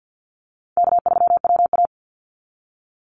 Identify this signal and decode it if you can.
Morse code, heard as a keyed tone, about 50 wpm, 700 Hz. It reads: K3WA
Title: Call Signs Repeated 2x - 50wpm